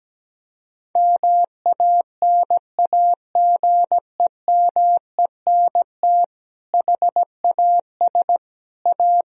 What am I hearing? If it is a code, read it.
Morse code, heard as a keyed tone, about 17 wpm, 700 Hz. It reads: MANAGEMENT HAS A
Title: Sentences from Top 1000 Words - Review Part 4 - 17wpm